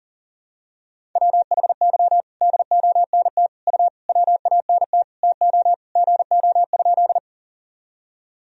Morse code, heard as a keyed tone, about 40 wpm, 700 Hz. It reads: WHY DONT U WANT TO GO?
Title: Sentences from Top 300 Words and Abbreviations 40wpm